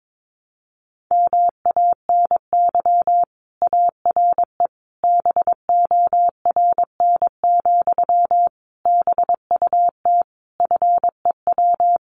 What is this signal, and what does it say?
Morse code, heard as a keyed tone, about 22 wpm, 700 Hz. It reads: MANY ARE BORN, BUT FEW
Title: Sentences from Top 1300 Words 22wpm